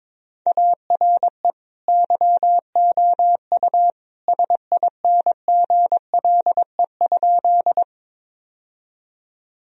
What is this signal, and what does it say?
Morse code, heard as a keyed tone, about 22 wpm, 700 Hz. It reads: ARE YOU SINGLE?
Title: Sentences from Top 1000 Words - Review Part 4 - 22wpm